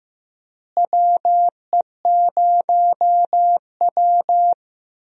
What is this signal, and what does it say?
Morse code, heard as a keyed tone, about 15 wpm, 700 Hz. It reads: WE0W
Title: USA Call Signs - 4 Characters - Repeated 3x - 15wpm